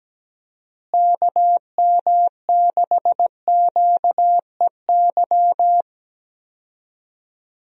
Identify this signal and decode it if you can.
Morse code, heard as a keyed tone, about 17 wpm, 700 Hz. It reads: KM6QEY
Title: USA Call Signs - 6 Characters - Repeated 2x - 17wpm